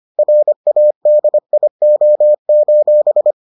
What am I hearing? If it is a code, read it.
Morse code, heard as a keyed tone, about 25 wpm, 600 Hz. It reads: RADIO: